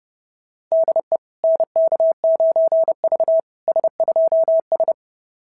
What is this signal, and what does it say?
Morse code, heard as a keyed tone, about 29 wpm, 690 Hz. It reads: DE NK9V S2S
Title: SOTA QSO Practice 30wpm